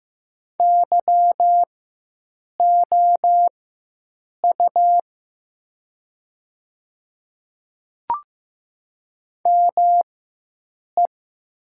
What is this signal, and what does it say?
Morse code, heard as a keyed tone, about 15 wpm, spaced as for 7 wpm, 700 Hz. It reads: YOU ME
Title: 4x Character Spacing - Top 200 Words - 15wpm